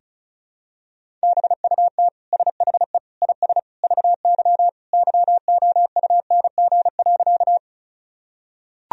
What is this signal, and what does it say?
Morse code, heard as a keyed tone, about 36 wpm, 700 Hz. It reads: BUT SHE IS VY YOUNG.
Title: Sentences from Top 400 Words and Abbreviations 35wpm